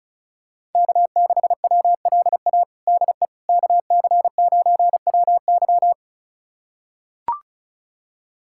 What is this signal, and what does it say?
Morse code, heard as a keyed tone, about 34 wpm, 690 Hz. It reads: K6WLA DE KC9WY
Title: USA Callsign DE Callsign 35wpm